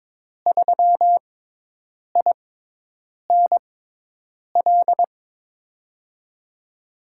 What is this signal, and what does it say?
Morse code, heard as a keyed tone, about 22 wpm, spaced as for 8 wpm, 700 Hz. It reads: 3INL
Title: USA Call Signs - 6 Characters - 6x Character Spacing - 22wpm